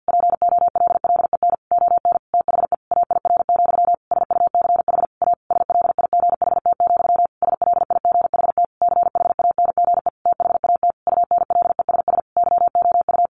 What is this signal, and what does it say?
Morse code, heard as a keyed tone, about 50 wpm, 700 Hz. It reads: POLLEN ON THE AIR, SUCH A SLIGHT, SLIGHT CHANGE THAT UNLESS YOU
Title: A Sound Of Thunder 50wpm